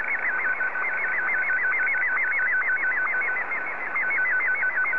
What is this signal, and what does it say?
SSTV (slow-scan television), Martin 1.